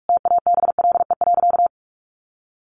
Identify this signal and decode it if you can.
Morse code, heard as a keyed tone, about 44 wpm, 700 Hz. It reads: TABLE.
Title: Sentences from Top 1400 Words 45wpm